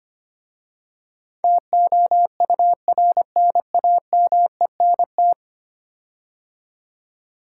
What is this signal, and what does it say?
Morse code, heard as a keyed tone, about 25 wpm, 700 Hz. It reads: TOURNAMENT
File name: 10 Letter Words - First 3 Letters Given - 25wpm